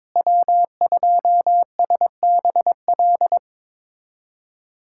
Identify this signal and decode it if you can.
Morse code, heard as a keyed tone, about 22 wpm, 700 Hz. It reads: W2SBL
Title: USA Callsign DE Callsign - 5 Characters - 22wpm